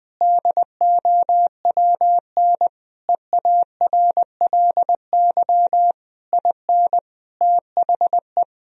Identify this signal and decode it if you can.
Morse code, heard as a keyed tone, about 20 wpm, 700 Hz. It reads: DOWN EARLY IN THE
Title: Sentences from Top 1000 Words - Review Part 2 - 20wpm